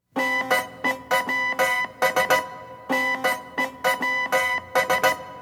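Morse code, heard as a keyed tone, about 16 wpm, 2000 Hz, with noise. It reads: NEWS NEWS
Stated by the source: Morse code used as music.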